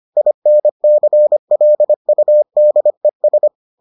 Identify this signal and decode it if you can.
Morse code, heard as a keyed tone, about 25 wpm, 600 Hz. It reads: INCLUDES